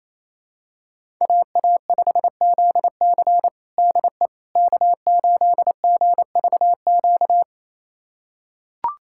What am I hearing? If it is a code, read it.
Morse code, heard as a keyed tone, about 28 wpm, 700 Hz. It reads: AA5ZC DE K8GVQ
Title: USA Callsign DE Callsign - 5 Characters - 28wpm